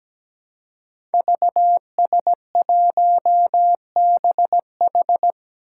Morse code, heard as a keyed tone, about 17 wpm, 700 Hz. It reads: VS1BH
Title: Five Challenging Letter-Number Combinations 17wpm